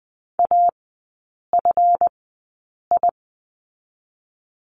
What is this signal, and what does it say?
Morse code, heard as a keyed tone, about 20 wpm, spaced as for 8 wpm, 700 Hz. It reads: AFI
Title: 138 Lesson 23e - Letter f - Three Characters 20wpm